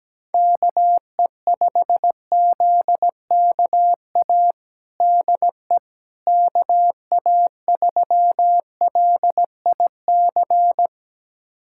Morse code, heard as a keyed tone, about 17 wpm, 700 Hz. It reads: KE5ZKA DE KA3LIC
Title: USA Callsign DE Callsign 17wpm